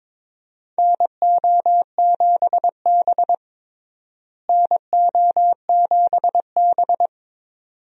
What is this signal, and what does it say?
Morse code, heard as a keyed tone, about 22 wpm, 700 Hz. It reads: NO7B NO7B
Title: USA Call Signs - 4 Characters - Repeated 2x - 22wpm